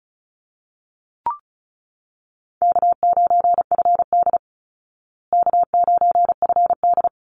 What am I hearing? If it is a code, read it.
Morse code, heard as a keyed tone, about 34 wpm, 700 Hz. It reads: K9FD K9FD
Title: Call Signs Repeated 3x - 35wpm